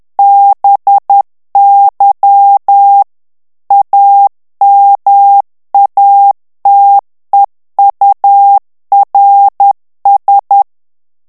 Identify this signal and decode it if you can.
Morse code, heard as a keyed tone, about 10 wpm, 800 Hz. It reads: BY AMATEURS
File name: ANGLAIS texte clair 3 vitesse 10 mots minute